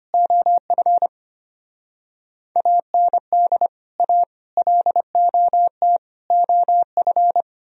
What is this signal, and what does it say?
Morse code, heard as a keyed tone, about 33 wpm, 700 Hz. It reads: OF AND A LOT OF